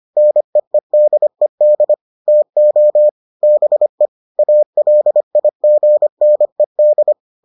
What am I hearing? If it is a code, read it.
Morse code, heard as a keyed tone, about 25 wpm, 600 Hz. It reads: NEEDED TO BE ALIGNED